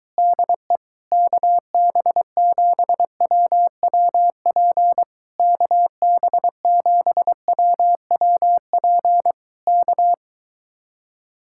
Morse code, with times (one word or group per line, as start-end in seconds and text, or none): 0.2-0.8 DE
1.1-5.0 KB7WWP
5.4-9.3 KB7WWP
9.7-10.1 K